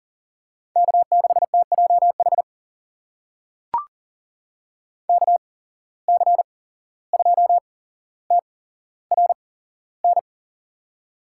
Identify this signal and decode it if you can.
Morse code, heard as a keyed tone, about 40 wpm, spaced as for 16 wpm, 700 Hz. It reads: K6TJH KC2TRN
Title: USA Call Signs - 8x Character Spacing - 40wpm